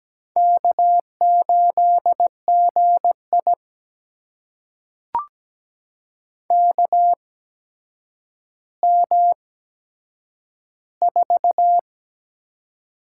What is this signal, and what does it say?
Morse code, heard as a keyed tone, about 17 wpm, spaced as for 8 wpm, 700 Hz. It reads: K8GI KM4
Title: USA Call Signs - 8x Character Spacing - 17wpm